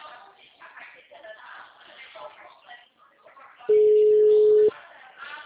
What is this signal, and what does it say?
Ringback tone (Europe, 425 Hz), with speech.